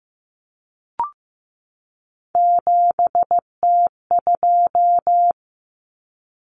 Morse code, heard as a keyed tone, about 15 wpm, 700 Hz. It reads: 7T2
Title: Common Cut Numbers - 3-Digit 15wpm